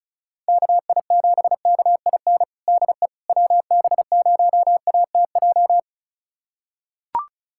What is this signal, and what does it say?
Morse code, heard as a keyed tone, about 34 wpm, 700 Hz. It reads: KI7KIN DE WB0ATJ
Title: USA Callsign DE Callsign 35wpm